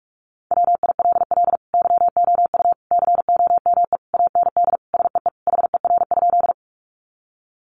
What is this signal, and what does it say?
Morse code, heard as a keyed tone, about 44 wpm, 700 Hz. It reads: WILL YOU COME AND SEE HER?
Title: Sentences from Top 100 Words 45wpm